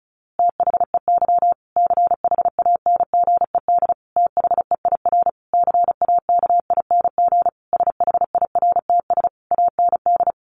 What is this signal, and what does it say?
Morse code, heard as a keyed tone, about 34 wpm, 700 Hz. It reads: THEY CHANGED THEIR CAKING SHIRTS AND
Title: A Sound Of Thunder 35wpm